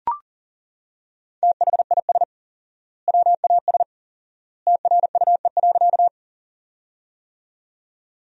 Morse code, heard as a keyed tone, about 40 wpm, 700 Hz. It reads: THIS WAS TRUE.
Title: 4x Word-Spacing - Sentences from Top 400 Words - 40wpm